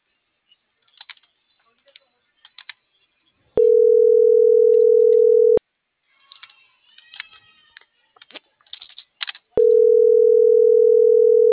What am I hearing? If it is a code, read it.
Ringback tone (North America), 440 + 480 Hz.